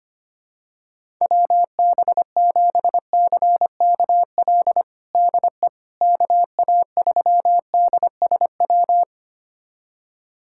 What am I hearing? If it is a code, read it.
Morse code, heard as a keyed tone, about 25 wpm, 700 Hz. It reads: WB7CKL DE KA3DSW
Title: USA Callsign DE Callsign - 6 Characters - 25wpm